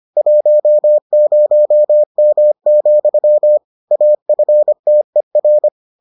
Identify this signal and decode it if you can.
Morse code, heard as a keyed tone, about 25 wpm, 600 Hz. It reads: 10M, AFTER